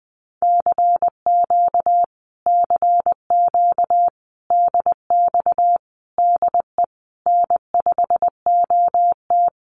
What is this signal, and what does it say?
Morse code, heard as a keyed tone, about 19 wpm, 700 Hz. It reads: CQ CQ DX DE N5OT
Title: Common QSO Phrases Encore 20wpm